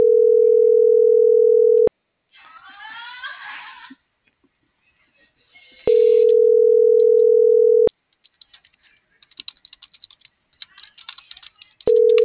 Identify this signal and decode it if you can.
Ringback tone (North America), 440 + 480 Hz.